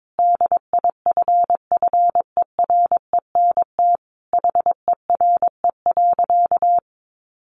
Morse code, heard as a keyed tone, about 22 wpm, 700 Hz. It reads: DIFFERENT HERE.
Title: Easy Phrases 22wpm